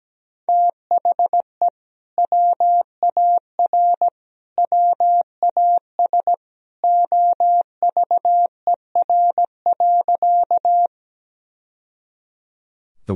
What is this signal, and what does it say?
Morse code, heard as a keyed tone, about 17 wpm, 700 Hz. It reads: THE WAR WAS OVER.
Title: Sentences from Top 1000 Words - Review Part 2 - 17wpm